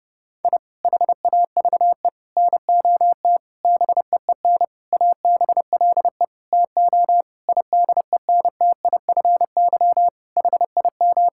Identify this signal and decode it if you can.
Morse code, heard as a keyed tone, about 32 wpm, 700 Hz. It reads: I HAVE NOT BEEN ABLE TO IDENTIFY HIM
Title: Sentences from Top 1000 Words - Review Part 6 - 30wpm